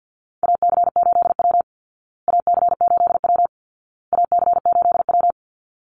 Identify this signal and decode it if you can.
Morse code, heard as a keyed tone, about 50 wpm, 700 Hz. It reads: AC8W AC8W AC8W
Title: Call Signs Repeated 2x - 50wpm